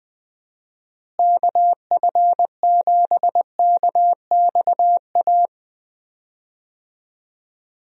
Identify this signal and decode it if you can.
Morse code, heard as a keyed tone, about 21 wpm, 700 Hz. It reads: KF7KXA
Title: USA Call Signs - 6 Characters - Repeated 3x - 20wpm